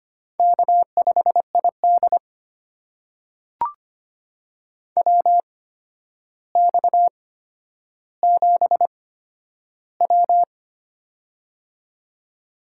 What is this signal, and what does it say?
Morse code, heard as a keyed tone, about 25 wpm, spaced as for 11 wpm, 700 Hz. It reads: K5ID WX7W
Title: USA Call Signs - 4 Characters - 8x Character Spacing - 25wpm